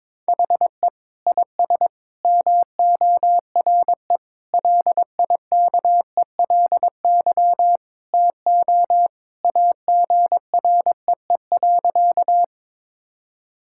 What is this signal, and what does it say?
Morse code, heard as a keyed tone, about 23 wpm, 700 Hz. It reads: HE IS MORE LIKELY TO AGREE.
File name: practice / 22wpm / Sentences with 3-Word Phrases - 22wpm